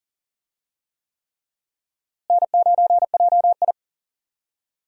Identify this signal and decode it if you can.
Morse code, heard as a keyed tone, about 40 wpm, 700 Hz. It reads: N9JI
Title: USA Call Signs - 4 Characters - Repeated 3x - 40wpm